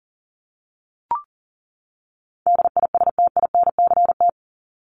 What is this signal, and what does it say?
Morse code, heard as a keyed tone, about 40 wpm, 700 Hz. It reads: DISTINCT
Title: Top 1501-1750 Words 40wpm